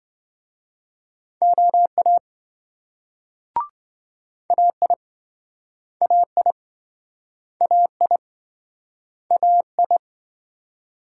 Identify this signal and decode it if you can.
Morse code, heard as a keyed tone, about 27 wpm, 700 Hz. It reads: OA AI AI AI AI